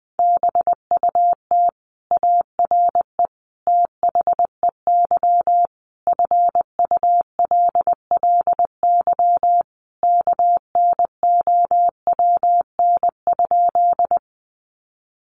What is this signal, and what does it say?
Morse code, heard as a keyed tone, about 20 wpm, 700 Hz. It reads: BUT ARE THEY FULLY KNOWN?
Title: Sentences from Top 1300 Words 20wpm